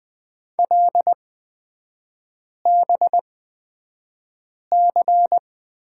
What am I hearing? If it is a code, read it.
Morse code, heard as a keyed tone, about 20 wpm, spaced as for 5 wpm, 700 Hz. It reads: LBC
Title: 243 Lesson 39e - Letter z - Three Characters 20wpm